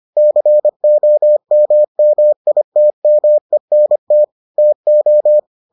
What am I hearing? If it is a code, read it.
Morse code, heard as a keyed tone, about 25 wpm, 600 Hz. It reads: COMMITMENT TO